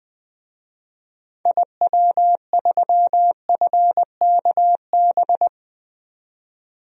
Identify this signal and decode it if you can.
Morse code, heard as a keyed tone, about 20 wpm, 700 Hz. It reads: IW3FKB
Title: DXCC-100 Call Signs - 6 Characters - 20wpm